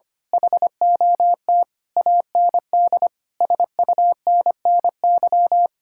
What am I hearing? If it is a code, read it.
Morse code, heard as a keyed tone, about 24 wpm, 700 Hz. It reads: HOT AND SUNNY
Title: Speed-Racing Common QSO Phrases - 25wpm